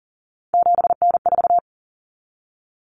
Morse code, heard as a keyed tone, about 40 wpm, 700 Hz. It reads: 7N4
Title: Common Cut Numbers - 3-Digit 40wpm